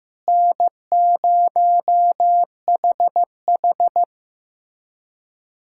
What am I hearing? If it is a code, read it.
Morse code, heard as a keyed tone, about 15 wpm, 700 Hz. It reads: N0HH
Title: USA Callsign DE Callsign - 4 Characters - 15wpm